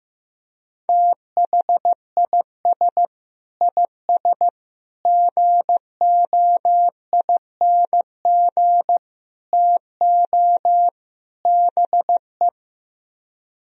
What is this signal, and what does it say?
Morse code, heard as a keyed tone, about 15 wpm, 700 Hz. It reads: THIS IS GOING TO BE
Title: Speed-Racing - Sets of 5 Words 15wpm